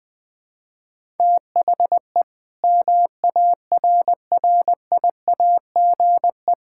Morse code, heard as a keyed tone, about 21 wpm, 700 Hz. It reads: THE MARRIAGE
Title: Sentences from Top 1000 Words - Review Part 5 - 20wpm